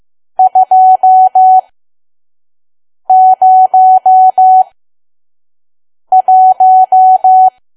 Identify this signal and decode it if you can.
Morse code, heard as a keyed tone, about 15 wpm, spaced as for 5 wpm, 750 Hz. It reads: 201
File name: Archive / 5 WPM files / PM